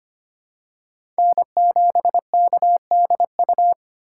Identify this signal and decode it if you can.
Morse code, heard as a keyed tone, about 25 wpm, 700 Hz. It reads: N7KDU
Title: USA Call Signs - 5 Characters - 3x Character Spacing - 25wpm